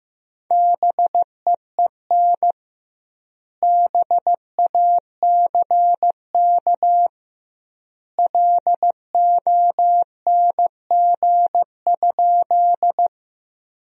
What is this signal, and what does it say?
Morse code, heard as a keyed tone, about 15 wpm, 700 Hz. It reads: BEEN BACK LONG?
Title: 2x Word-Spacing - Sentences from Top 200 Words - 15wpm